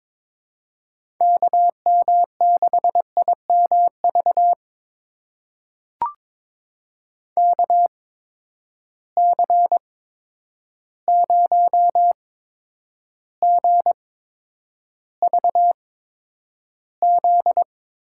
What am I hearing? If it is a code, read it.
Morse code, heard as a keyed tone, about 21 wpm, spaced as for 9 wpm, 700 Hz. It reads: KM6IMV KC0GVZ
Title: USA Call Signs - 6 Characters - 8x Character Spacing - 22wpm